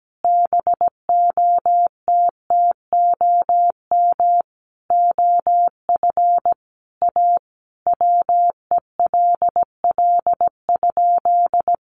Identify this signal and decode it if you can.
Morse code, heard as a keyed tone, about 17 wpm, 700 Hz. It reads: BOTTOM OF A WELL?
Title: Sentences from Top 900 Words 17wpm